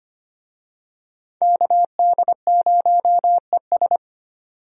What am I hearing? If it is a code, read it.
Morse code, heard as a keyed tone, about 25 wpm, 700 Hz. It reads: KD0ES
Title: USA Call Signs - 5 Characters - 25wpm